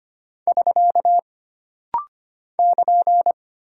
Morse code, heard as a keyed tone, about 25 wpm, 700 Hz. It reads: <SK> <KN>
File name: Single Punctuation-Prosign - Rapid-Fire - 25wpm